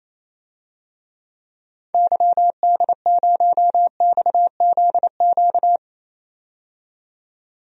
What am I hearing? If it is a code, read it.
Morse code, heard as a keyed tone, about 28 wpm, 700 Hz. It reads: YD0XZQ
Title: DXCC-100 Call Signs - 6 Characters - 28wpm